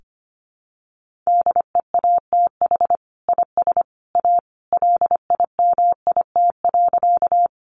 Morse code, heard as a keyed tone, about 25 wpm, 700 Hz. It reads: DEATH IS A LIMIT.
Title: Sentences from Top 1500 Words 25wpm